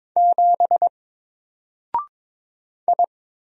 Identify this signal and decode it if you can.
Morse code, heard as a keyed tone, about 22 wpm, 700 Hz. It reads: 7 I
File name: Single-Letter-Number-22wpm